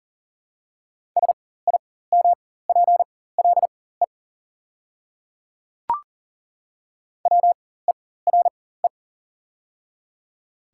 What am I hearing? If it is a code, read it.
Morse code, heard as a keyed tone, about 40 wpm, spaced as for 19 wpm, 700 Hz. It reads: SIMPLE WERE
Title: 4x Character Spacing - Top 500 Words - 40wpm